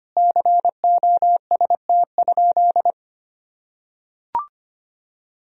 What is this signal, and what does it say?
Morse code, heard as a keyed tone, about 25 wpm, 700 Hz. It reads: COST?
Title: Sentences from Top 1000 Words - Review Part 4 - 25wpm